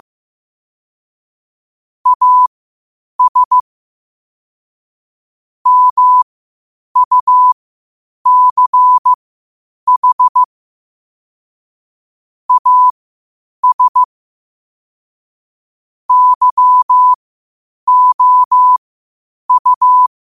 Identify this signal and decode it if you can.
Morse code, heard as a keyed tone, about 15 wpm, spaced as for 8 wpm, 1000 Hz. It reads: AS MUCH AS YOU
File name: qotdfiles / 8wpm